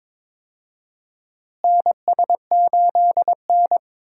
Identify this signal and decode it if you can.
Morse code, heard as a keyed tone, about 22 wpm, 700 Hz. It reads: NS8N
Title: USA Call Signs - 4 Characters - Repeated 3x - 22wpm